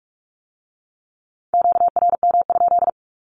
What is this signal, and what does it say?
Morse code, heard as a keyed tone, about 46 wpm, 700 Hz. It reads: QRM?
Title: Q-Codes with Spoken Meaning - 45wpm